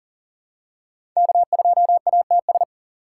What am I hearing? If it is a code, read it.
Morse code, heard as a keyed tone, about 40 wpm, 700 Hz. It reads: K2ATS
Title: USA Call Signs - 5 Characters - 4x Character Spacing - 40wpm